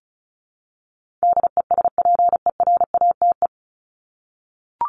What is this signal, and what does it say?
Morse code, heard as a keyed tone, about 34 wpm, 700 Hz. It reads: DESPERATE
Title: Top 1751-2000 Words 35wpm